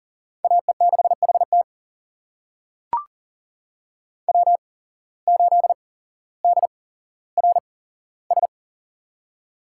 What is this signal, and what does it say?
Morse code, heard as a keyed tone, about 40 wpm, spaced as for 17 wpm, 700 Hz. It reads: AE6HT W8DRS
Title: USA Call Signs - 5 Characters - 8x Character Spacing - 40wpm